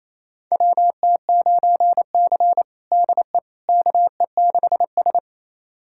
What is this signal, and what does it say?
Morse code, heard as a keyed tone, about 28 wpm, 700 Hz. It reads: WT9C DE KE6S
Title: USA Callsign DE Callsign - 4 Characters - 28wpm